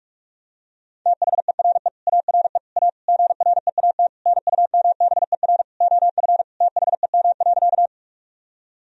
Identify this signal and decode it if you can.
Morse code, heard as a keyed tone, about 46 wpm, 700 Hz. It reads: THERE ARE A GREAT NUMBER OF THEM.
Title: Sentences from Top 1000 Words - Review Part 1 - 45wpm